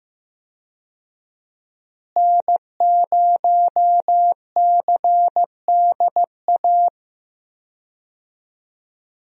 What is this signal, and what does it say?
Morse code, heard as a keyed tone, about 16 wpm, 700 Hz. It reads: N0CDA
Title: USA Call Signs - 5 Characters - Repeated 3x - 15wpm